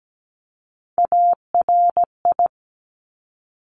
Morse code, heard as a keyed tone, about 17 wpm, 700 Hz. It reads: ARI
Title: Top Three Letter Combinations in Words 17wpm